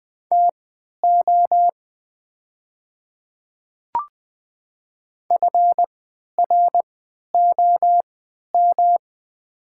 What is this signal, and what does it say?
Morse code, heard as a keyed tone, about 20 wpm, spaced as for 11 wpm, 700 Hz. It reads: TO FROM
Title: 3x Character Spacing - Top 300 Words - 20wpm